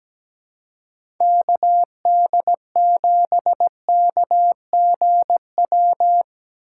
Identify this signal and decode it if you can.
Morse code, heard as a keyed tone, about 17 wpm, 700 Hz. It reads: KD7KGW
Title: USA Call Signs - 6 Characters - 2x Character Spacing - 17wpm